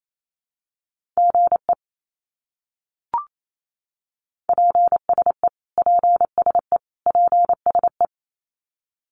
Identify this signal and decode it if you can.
Morse code, heard as a keyed tone, about 28 wpm, 700 Hz. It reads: GE PSE PSE PSE
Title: Abbreviations Repeated 3x - 28wpm